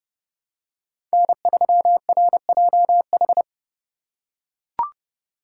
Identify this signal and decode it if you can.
Morse code, heard as a keyed tone, about 30 wpm, 700 Hz. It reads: N3RJH
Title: USA Call Signs - 5 Characters - 4x Character Spacing - 30wpm